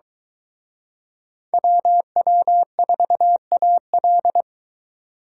Morse code, heard as a keyed tone, about 23 wpm, 700 Hz. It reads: WW4AL